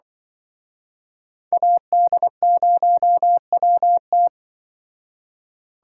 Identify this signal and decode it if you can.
Morse code, heard as a keyed tone, about 25 wpm, 700 Hz. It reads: AD0WT
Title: Speed Racing - USA 2x2 Call Signs 20wpm